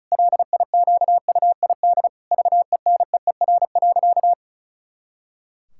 Morse code, heard as a keyed tone, about 34 wpm, 700 Hz. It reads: LIQUID VENEER.
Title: Pangram Challenge v2 - 35wpm